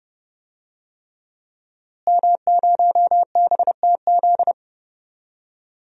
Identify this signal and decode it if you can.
Morse code, heard as a keyed tone, about 31 wpm, 700 Hz. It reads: M0BTZ
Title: DXCC-100 Call Signs - 5 Characters - Repeated 3x - 30wpm